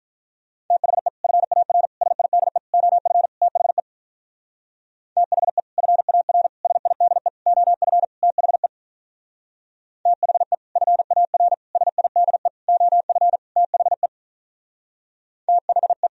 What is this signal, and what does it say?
Morse code, heard as a keyed tone, about 44 wpm, 700 Hz. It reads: THE FAR SIDE OF THE THE FAR SIDE OF THE THE FAR SIDE OF THE THE
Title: Speed-Racing - Sets of 5 Words 35wpm